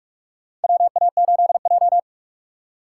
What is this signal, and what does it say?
Morse code, heard as a keyed tone, about 42 wpm, 700 Hz. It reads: WA8J
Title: USA Call Signs - 4x Character Spacing - 45wpm